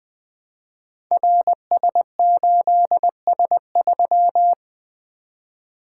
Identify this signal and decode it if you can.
Morse code, heard as a keyed tone, about 20 wpm, 700 Hz. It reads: RS8S3
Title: Code Groups - 5 Letter:Numbers - 20wpm